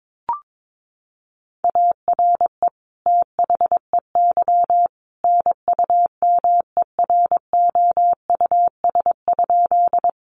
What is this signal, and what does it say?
Morse code, heard as a keyed tone, about 22 wpm, 700 Hz. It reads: ARE THEY NUMEROUS?
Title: Sentences from Top 1400 Words 22wpm